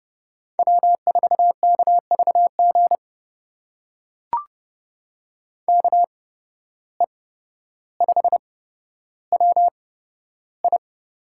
Morse code, heard as a keyed tone, about 30 wpm, spaced as for 13 wpm, 700 Hz. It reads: W4KVG KE5WI
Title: USA Call Signs - 5 Characters - 8x Character Spacing - 30wpm